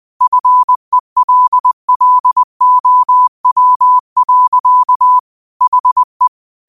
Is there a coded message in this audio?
Yes: Morse code from 0.2 to 6.3 s, heard as a keyed tone, about 20 wpm, 1000 Hz. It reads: FELLOW. HE